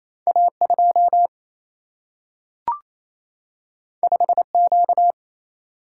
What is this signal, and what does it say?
Morse code, heard as a keyed tone, about 28 wpm, 700 Hz. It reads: A2 5Q
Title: Single-Letter Single-Number 28wpm